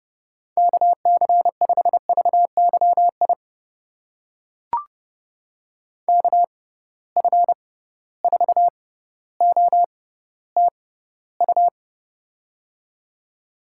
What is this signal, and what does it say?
Morse code, heard as a keyed tone, about 30 wpm, spaced as for 15 wpm, 700 Hz. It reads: KC5VYI KF4OTU
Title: USA Call Signs - 6 Characters - 6x Character Spacing - 30wpm